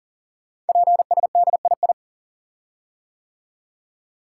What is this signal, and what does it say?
Morse code, heard as a keyed tone, about 38 wpm, 700 Hz. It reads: PSDII
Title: Code Groups - 5 Letter:Numbers - 40wpm